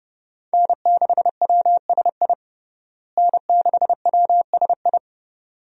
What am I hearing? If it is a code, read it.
Morse code, heard as a keyed tone, about 32 wpm, 700 Hz. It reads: N6WSI N6WSI
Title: USA Call Signs - 5 Characters - Repeated 2x - 30wpm